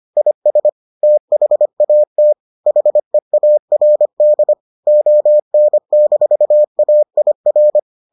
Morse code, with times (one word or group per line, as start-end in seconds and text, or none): 0.2-0.7 IS
1.0-2.3 THAT
2.7-4.5 HEARD
4.9-7.8 ON-AIR